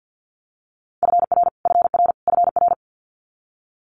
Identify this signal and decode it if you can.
Morse code, heard as a keyed tone, about 50 wpm, 700 Hz. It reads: FR FR FR
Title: Multi-letter Phonemes Repeated 3x 50wpm